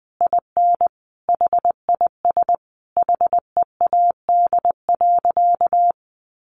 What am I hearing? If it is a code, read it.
Morse code, heard as a keyed tone, about 20 wpm, 700 Hz. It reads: IN HIS HEAD.
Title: Sentences from Top 1300 Words 20wpm